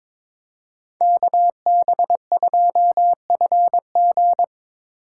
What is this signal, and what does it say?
Morse code, heard as a keyed tone, about 25 wpm, 700 Hz. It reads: KB2FG
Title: Speed Racing - USA 2x2 Call Signs 22wpm